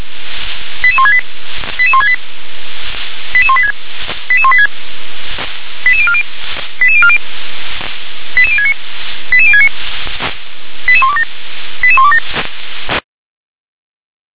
Five-tone selective call (ZVEI), heard as quick runs of five tones, tones of about 70 ms. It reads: 88168 88168 88186 88186 88040 88040 88069 88069 88127 88127